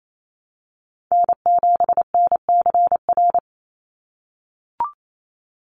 Morse code, heard as a keyed tone, about 28 wpm, 700 Hz. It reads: N7NCR
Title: USA Call Signs 28wpm